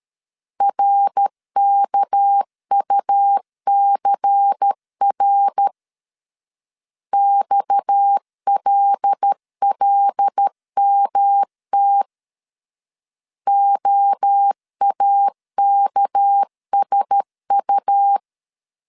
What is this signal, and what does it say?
Morse code, heard as a keyed tone, about 13 wpm, 800 Hz. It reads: RKUCR XLLMT OAKSU
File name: CW / DET / 1 6 1500C S A 12wpm 3440